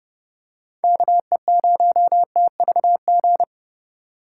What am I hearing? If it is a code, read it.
Morse code, heard as a keyed tone, about 29 wpm, 700 Hz. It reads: KE0TVG